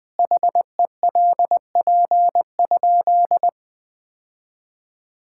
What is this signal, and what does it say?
Morse code, heard as a keyed tone, about 21 wpm, 700 Hz. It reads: HELP?
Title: Sentences with 9-Word Phrases - 20wpm